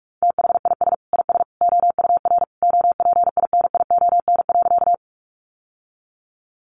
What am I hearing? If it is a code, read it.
Morse code, heard as a keyed tone, about 44 wpm, 700 Hz. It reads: THIS IS OUR OPINION.
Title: Sentences from Top 1200 Words 45wpm